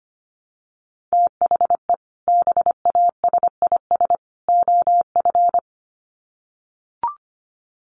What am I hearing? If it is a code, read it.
Morse code, heard as a keyed tone, about 25 wpm, 700 Hz. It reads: THE BASIS OF
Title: Sets of 3 Words 25wpm